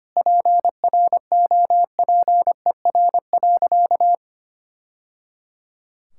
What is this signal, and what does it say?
Morse code, heard as a keyed tone, about 25 wpm, 700 Hz. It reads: PROPER.
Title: Sentences from Top 1000 Words - Review Part 5 - 25wpm